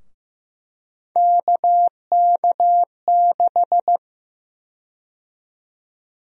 Morse code, heard as a keyed tone, about 15 wpm, 700 Hz. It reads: KK6
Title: USA Call Sign Prefixes - Repeated 2x - 15wpm